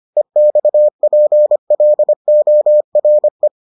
Morse code, heard as a keyed tone, about 25 wpm, 600 Hz. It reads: EXPLORE